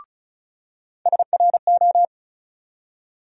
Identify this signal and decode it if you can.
Morse code, heard as a keyed tone, about 34 wpm, 700 Hz. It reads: SRO
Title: USA Call Sign Postfixes - 3 Letters 35wpm